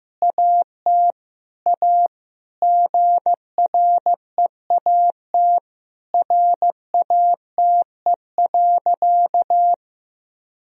Morse code, heard as a keyed tone, about 15 wpm, 700 Hz. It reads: AT A GREAT RATE.
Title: Sentences from Top 1000 Words - Review Part 4 - 15wpm